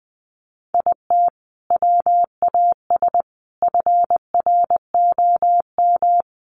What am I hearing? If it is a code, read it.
Morse code, heard as a keyed tone, about 20 wpm, 700 Hz. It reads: IT WAS FROM
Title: Sentences from Top 100 Words 20wpm